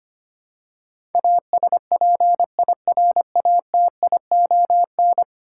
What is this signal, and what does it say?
Morse code, heard as a keyed tone, about 25 wpm, 700 Hz. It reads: ASPIRATION
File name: 10 Letter Words - First 3 Letters Given - 25wpm